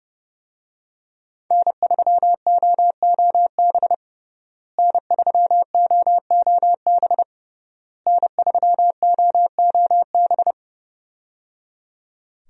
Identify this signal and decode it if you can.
Morse code, heard as a keyed tone, about 30 wpm, 700 Hz. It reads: N3OOB N3OOB N3OOB
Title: USA Call Signs - 5 Characters - Repeated 3x - 30wpm